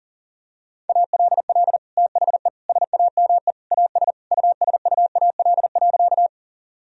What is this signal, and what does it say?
Morse code, heard as a keyed tone, about 40 wpm, 700 Hz. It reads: ALL THE SAME AS USUAL.
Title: Sentences from Top 1000 Words - Review Part 2 - 40wpm